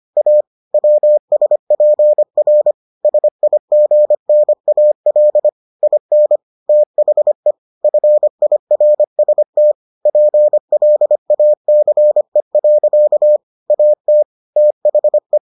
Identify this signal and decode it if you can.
Morse code, heard as a keyed tone, about 25 wpm, 600 Hz. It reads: A WSPR SIGNAL IN THE FIRST PLACE. AT THE